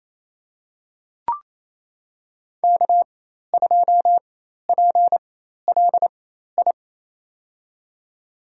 Morse code, heard as a keyed tone, about 27 wpm, spaced as for 13 wpm, 700 Hz. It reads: K2PLI
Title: USA Call Signs - 5 Characters - 4x Character Spacing - 28wpm